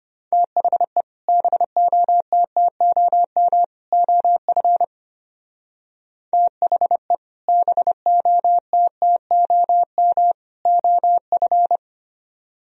Morse code, heard as a keyed tone, about 31 wpm, 700 Hz. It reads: THE BOTTOM OF THE BOTTOM OF
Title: Speed-Racing - Sets of 3 Words 25wpm